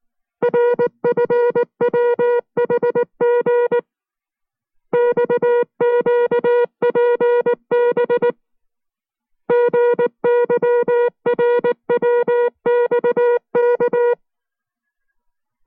Morse code, heard as a keyed tone, about 19 wpm, spaced as for 16 wpm, 480 Hz. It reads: RFWHG XQPBG YRWXK
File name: FreeFiles / CW-practice / 15WPM-side2